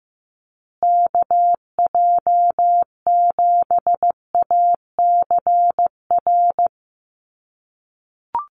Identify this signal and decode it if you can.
Morse code, heard as a keyed tone, about 15 wpm, 700 Hz. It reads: KJ7ACR
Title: USA Call Signs Repeated 2x 15wpm